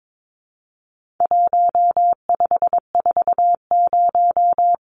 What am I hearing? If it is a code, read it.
Morse code, heard as a keyed tone, about 22 wpm, 700 Hz. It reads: 1540
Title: Numbers - 4-Digit 22wpm